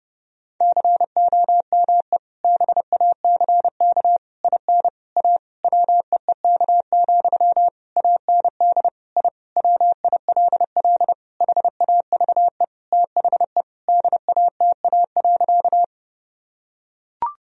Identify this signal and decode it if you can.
Morse code, heard as a keyed tone, about 29 wpm, 700 Hz. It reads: COME BACK IN A WEEK, AND I WILL HAVE THE DATA.
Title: Sentences from Top 1000 Words - Review Part 3 - 30wpm